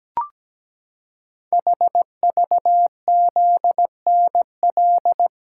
Morse code, heard as a keyed tone, about 17 wpm, 700 Hz. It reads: HVZNL
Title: Code Groups - 5 Letters - 17wpm